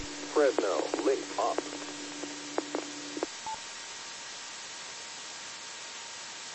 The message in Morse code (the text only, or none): none